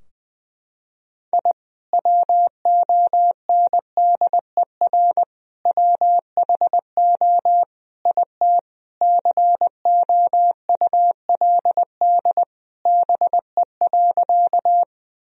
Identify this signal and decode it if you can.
Morse code, heard as a keyed tone, about 21 wpm, 700 Hz. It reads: I WONDER WHO IT COULD BE.
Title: Sentences from Top 1000 Words - Review Part 3 - 20wpm